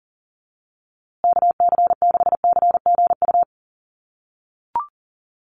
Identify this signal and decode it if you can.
Morse code, heard as a keyed tone, about 40 wpm, 700 Hz. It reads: KC6CGU
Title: USA Call Signs Repeated 2x 40wpm